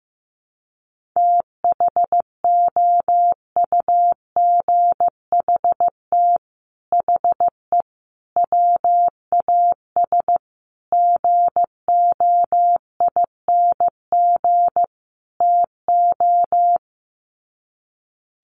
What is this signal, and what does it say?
Morse code, heard as a keyed tone, about 15 wpm, 700 Hz. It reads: THOUGHT HE WAS GOING TO
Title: Sets of 5 Words Encore2 15wpm